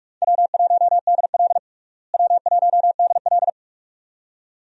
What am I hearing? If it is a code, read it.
Morse code, heard as a keyed tone, about 46 wpm, 700 Hz. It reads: W1DL W1DL
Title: USA Call Signs - 4 Characters - Repeated 2x - 45wpm